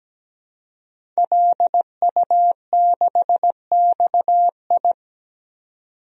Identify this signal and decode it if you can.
Morse code, heard as a keyed tone, about 17 wpm, 700 Hz. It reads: LU6XI
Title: DXCC-100 Call Signs - 5 Characters - 17wpm